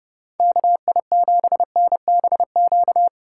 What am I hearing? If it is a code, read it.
Morse code, heard as a keyed tone, about 30 wpm, 700 Hz. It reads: KI7NBQ